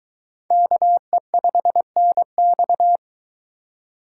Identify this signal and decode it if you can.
Morse code, heard as a keyed tone, about 21 wpm, 700 Hz. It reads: KE5NX